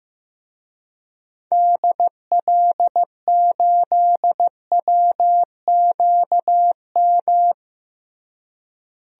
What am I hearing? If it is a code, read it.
Morse code, heard as a keyed tone, about 15 wpm, 700 Hz. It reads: DL8WQM
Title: DXCC-100 Call Signs - 6 Characters - Repeated 2x - 15wpm